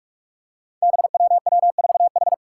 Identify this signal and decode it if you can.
Morse code, heard as a keyed tone, about 43 wpm, 700 Hz. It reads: BWW4H
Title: Five Challenging Letter-Number Combinations 45wpm